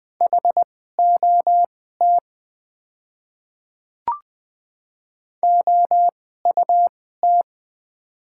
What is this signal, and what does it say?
Morse code, heard as a keyed tone, about 20 wpm, spaced as for 14 wpm, 700 Hz. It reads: HOT OUT
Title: 2x Character Spacing - Top 100 Words - 20wpm